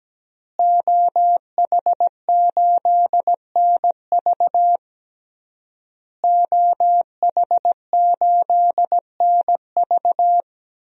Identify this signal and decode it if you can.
Morse code, heard as a keyed tone, about 17 wpm, 700 Hz. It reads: OH8NV OH8NV
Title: DXCC-100 Call Signs - 5 Characters - Repeated 3x - 17wpm